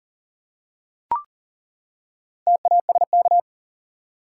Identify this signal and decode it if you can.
Morse code, heard as a keyed tone, about 40 wpm, 700 Hz. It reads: TASK
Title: Four Letter Words - 40wpm